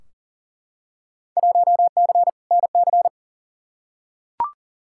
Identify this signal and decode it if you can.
Morse code, heard as a keyed tone, about 40 wpm, 700 Hz. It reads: 1C NC
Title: Field Day Exchanges - 40wpm